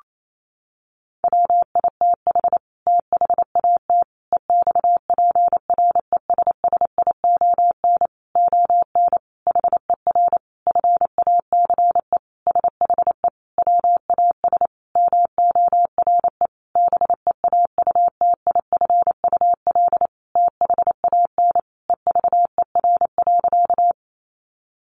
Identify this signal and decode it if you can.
Morse code, heard as a keyed tone, about 28 wpm, 700 Hz. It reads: WITH THAT EXPRESSION ON HER FACE SHE WAS MORE BEAUTIFUL THAN EVER.
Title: Sentences from Top 1200 Words 28wpm